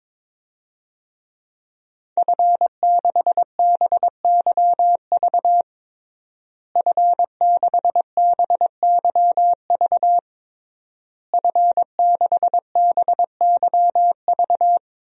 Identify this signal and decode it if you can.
Morse code, heard as a keyed tone, about 22 wpm, 700 Hz. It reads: F6BYV F6BYV F6BYV
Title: DXCC-100 Call Signs - 5 Characters - Repeated 3x - 22wpm